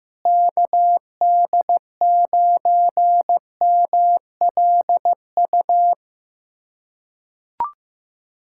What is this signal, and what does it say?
Morse code, heard as a keyed tone, about 16 wpm, 700 Hz. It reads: KD9MLU
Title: USA Callsign DE Callsign - 6 Characters - 15wpm